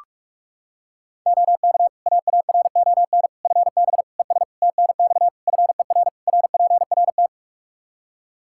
Morse code, heard as a keyed tone, about 46 wpm, 700 Hz. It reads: OK AARON FB ES TNX FER RPRT
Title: K1USN SST QSO Practice 45wpm